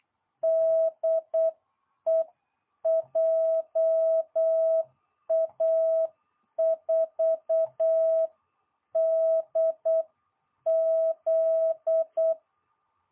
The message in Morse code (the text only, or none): DE JA4DZ